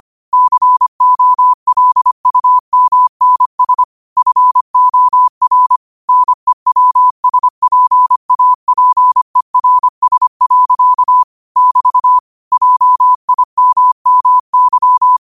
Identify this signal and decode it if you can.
Morse code, heard as a keyed tone, about 25 wpm, 1000 Hz. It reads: COLUMNS FOR NEWSPAPERS. <BT> JIMMY
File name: qotdfiles / 25wpm